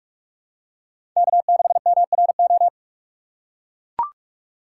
Morse code, heard as a keyed tone, about 42 wpm, 700 Hz. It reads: K6MRO
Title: USA Call Signs - 5 Characters - 3x Character Spacing - 45wpm